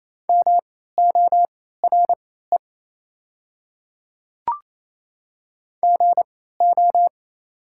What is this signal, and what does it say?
Morse code, heard as a keyed tone, about 27 wpm, spaced as for 16 wpm, 690 Hz. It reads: MORE GO
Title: 3x Character Spacing - Top 100 Words - 28wpm